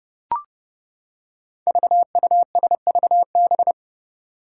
Morse code, heard as a keyed tone, about 29 wpm, 700 Hz. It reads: VUSVB
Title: Five Challenging Letter-Number Combinations 30wpm